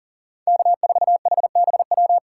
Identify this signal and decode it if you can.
Morse code, heard as a keyed tone, about 40 wpm, 700 Hz. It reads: K4HBW
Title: USA Call Signs - 5 Characters - 3x Character Spacing - 40wpm